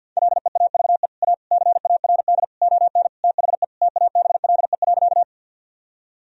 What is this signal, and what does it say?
Morse code, heard as a keyed tone, about 50 wpm, 700 Hz. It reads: LEAVE A CARD ON THE TABLE.
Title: Sentences from Top 1000 Words - Review Part 4 - 50wpm